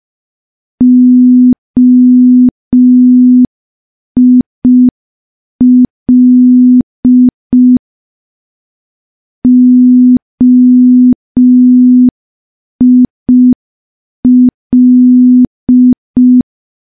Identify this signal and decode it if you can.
Morse code, heard as a keyed tone, about 5 wpm, 250 Hz. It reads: OIL OIL
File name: morse / morse-training-05WPM